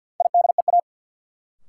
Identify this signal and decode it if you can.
Morse code, heard as a keyed tone, about 52 wpm, 700 Hz. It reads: IDEA
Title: Sets of 3 Words - New Word Drill - 50wpm